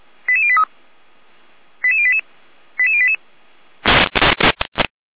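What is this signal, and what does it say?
Five-tone selective call (ZVEI), heard as quick runs of five tones, tones of about 70 ms. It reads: 88083 88088 88088
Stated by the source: Five-tone ZVEI selective call 88098, received by SDR.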